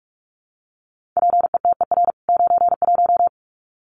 Morse code, heard as a keyed tone, about 44 wpm, 700 Hz. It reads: PETER 91
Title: CWT Exchange 45wpm